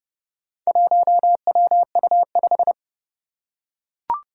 Morse code, heard as a keyed tone, about 30 wpm, 700 Hz. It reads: 1WU5
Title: Four Challenging Letter-Number Combinations 30wpm